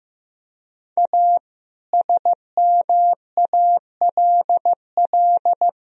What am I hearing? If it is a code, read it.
Morse code, heard as a keyed tone, about 15 wpm, 700 Hz. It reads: A SMALL
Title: Sentences from Top 1000 Words - Review Part 1 - 15wpm